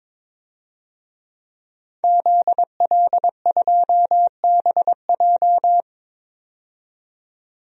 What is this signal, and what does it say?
Morse code, heard as a keyed tone, about 23 wpm, 700 Hz. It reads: ZL2BJ
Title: DXCC-100 Call Signs - 5 Characters - 22wpm